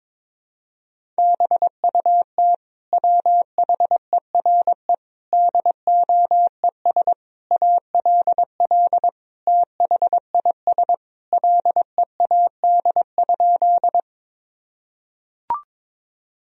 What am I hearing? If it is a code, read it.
Morse code, heard as a keyed tone, about 22 wpm, 700 Hz. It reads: BUT WHERE DOES ALL THIS LEAD?
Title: Sentences from Top 1000 Words - Review Part 2 - 22wpm